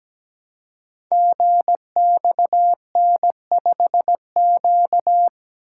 Morse code, heard as a keyed tone, about 17 wpm, 700 Hz. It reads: GXN5Q
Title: Code Groups - 5 Letter:Numbers - 17wpm